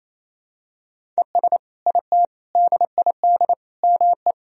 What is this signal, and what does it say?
Morse code, heard as a keyed tone, about 28 wpm, 700 Hz. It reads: ES IT DID ME
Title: Sentences from Top 200 Words and Abbreviations 28wpm